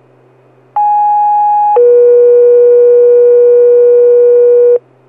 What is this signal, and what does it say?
Two-tone page, heard as a one-second tone, then a three-second tone, about 830 then 480 Hz. It is heard from 0.8 to 4.8 s.